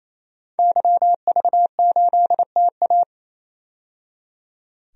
Morse code, heard as a keyed tone, about 28 wpm, 700 Hz. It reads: YV8TA
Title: DXCC-100 Call Signs - 5 Characters - 28wpm